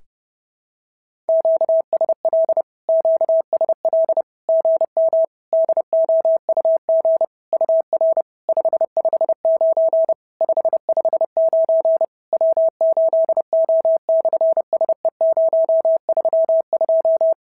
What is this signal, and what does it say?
Morse code, heard as a keyed tone, about 30 wpm, 650 Hz. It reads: QSL QSL GM DOUG UR 559 559 W8O/SE032
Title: SOTA QSO Practice 30wpm